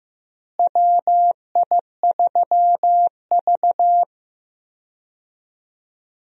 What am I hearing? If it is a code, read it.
Morse code, heard as a keyed tone, about 15 wpm, 700 Hz. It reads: WI3V
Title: USA Call Signs - 4 Characters - Repeated 2x - 15wpm